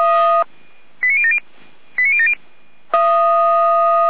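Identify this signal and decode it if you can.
Five-tone selective call (ZVEI), heard as quick runs of five tones, tones of about 70 ms. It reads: 89070 89070
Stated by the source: ZVEI five-tone selective call received by SDR.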